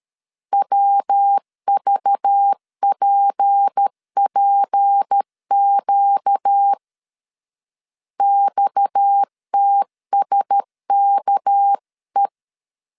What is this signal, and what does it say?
Morse code, heard as a keyed tone, about 13 wpm, 800 Hz. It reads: WVPPQ XTSKE
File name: CW / DET / 1 6 1500C S B 12wpm 3431